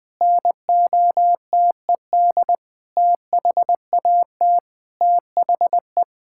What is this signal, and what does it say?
Morse code, heard as a keyed tone, about 20 wpm, 700 Hz. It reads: NOTED THAT THE
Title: Sentences with 7-Word Phrases - 20wpm